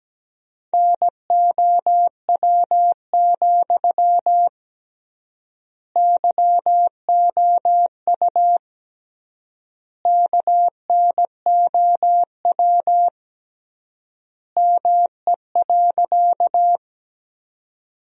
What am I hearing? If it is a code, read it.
Morse code, heard as a keyed tone, about 17 wpm, 700 Hz. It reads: NOW, YOU KNOW ME.
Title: 3x Word-Spacing - Sentences from Top 200 Words - 17wpm